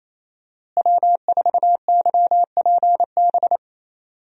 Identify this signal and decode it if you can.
Morse code, heard as a keyed tone, about 28 wpm, 700 Hz. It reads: W4YPB
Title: USA Call Signs - 5 Characters - Repeated 2x - 28wpm